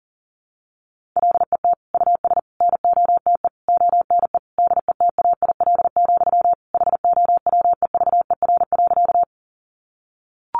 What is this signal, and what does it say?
Morse code, heard as a keyed tone, about 37 wpm, 700 Hz. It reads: LET US NOTE ONE DETAIL, HOWEVER.
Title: Sentences from Top 1500 Words 40wpm